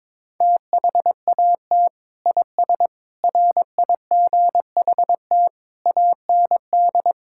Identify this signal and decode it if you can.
Morse code, heard as a keyed tone, about 22 wpm, 700 Hz. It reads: THAT IS RIGHT AND
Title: Sentences from Top 1000 Words - Review Part 5 - 22wpm